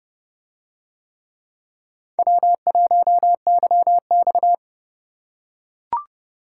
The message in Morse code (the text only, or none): W1YX